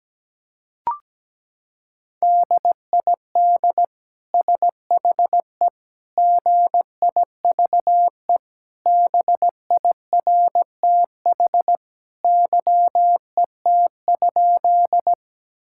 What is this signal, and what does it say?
Morse code, heard as a keyed tone, about 17 wpm, 700 Hz. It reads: DID SHE GIVE BIRTH YET?
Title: Sentences from Top 1000 Words - Review Part 5 - 17wpm